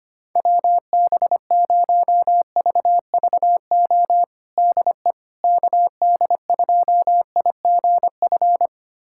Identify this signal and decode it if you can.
Morse code, heard as a keyed tone, about 25 wpm, 700 Hz. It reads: WB0VVO DE KD2IGF
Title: USA Callsign DE Callsign - 6 Characters - 25wpm